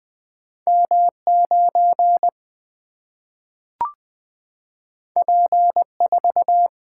Morse code, heard as a keyed tone, about 20 wpm, 700 Hz. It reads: M 9 P 4
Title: Single-Letter Single-Number 20wpm